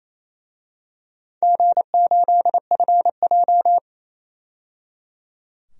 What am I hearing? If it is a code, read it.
Morse code, heard as a keyed tone, about 28 wpm, 700 Hz. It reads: G8FJ
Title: DXCC-100 Call Signs - 4 Characters - 28wpm